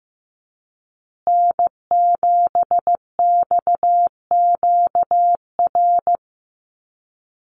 Morse code, heard as a keyed tone, about 15 wpm, 700 Hz. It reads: N7XQR
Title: USA Call Signs Repeated 2x 15wpm